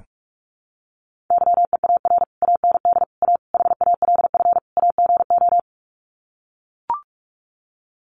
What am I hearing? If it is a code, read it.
Morse code, heard as a keyed tone, about 44 wpm, 700 Hz. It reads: YEAR AND A HALF AGO
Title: Sets of 5 Words Encore 45wpm